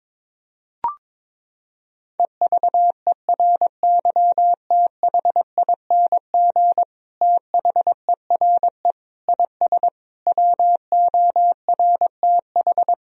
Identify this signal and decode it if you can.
Morse code, heard as a keyed tone, about 22 wpm, 700 Hz. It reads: EVERYTHING THERE IS WORTH
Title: Sentences from Top 1000 Words - Review Part 1 - 22wpm